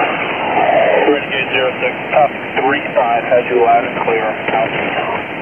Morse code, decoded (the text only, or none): none